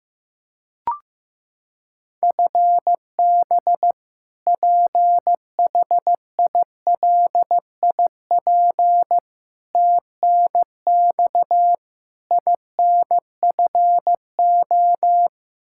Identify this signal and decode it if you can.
Morse code, heard as a keyed tone, about 15 wpm, 700 Hz. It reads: FB PHILIP TNX INFO
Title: Practice QSO - Exchange 3 - More Info - 15wpm